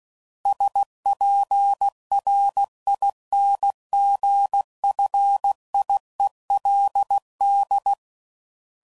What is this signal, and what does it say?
Morse code, heard as a keyed tone, about 16 wpm, 800 Hz. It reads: SPRINGFIELD
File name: CW Code Groups @15WPM / fromSM's / Cities